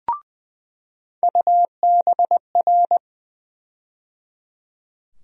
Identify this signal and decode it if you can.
Morse code, heard as a keyed tone, about 20 wpm, 700 Hz. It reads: UBR
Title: USA Call Sign Postfixes - 3 Letters 20wpm